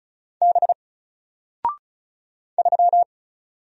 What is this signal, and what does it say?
Morse code, heard as a keyed tone, about 35 wpm, 700 Hz. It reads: B 3
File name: Single-Letter-Number-35wpm